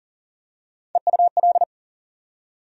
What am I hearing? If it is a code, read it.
Morse code, heard as a keyed tone, about 40 wpm, 700 Hz. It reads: EUL
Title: USA Call Sign Postfixes - 3 Letters 40wpm